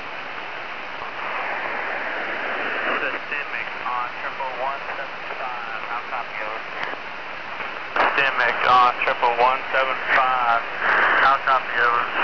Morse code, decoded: none

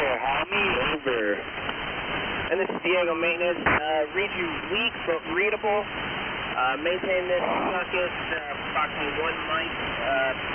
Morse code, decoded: none

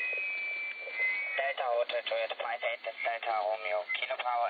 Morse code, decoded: none

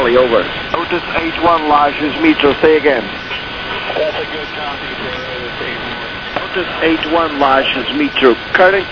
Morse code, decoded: none

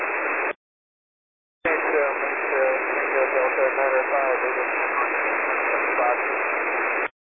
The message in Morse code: none

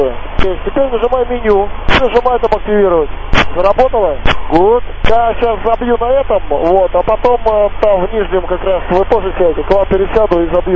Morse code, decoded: none

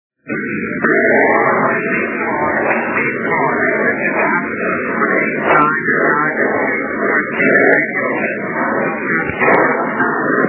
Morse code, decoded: none

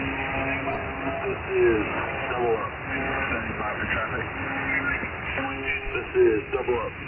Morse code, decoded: none

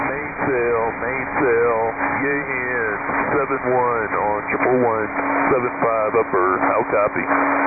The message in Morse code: none

